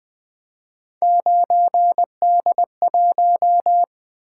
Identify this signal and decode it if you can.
Morse code, heard as a keyed tone, about 20 wpm, 700 Hz. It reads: 9D1
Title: Three Letter-Number Combinations - 20wpm